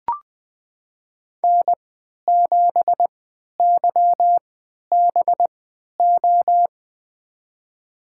Morse code, heard as a keyed tone, about 21 wpm, spaced as for 11 wpm, 700 Hz. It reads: N7YBO
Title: USA Call Signs - 5 Characters - 3x Character Spacing - 20wpm